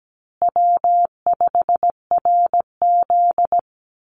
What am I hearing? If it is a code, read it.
Morse code, heard as a keyed tone, about 17 wpm, 700 Hz. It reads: W5RZ
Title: Easy Call Signs Repeated 2x - 17wpm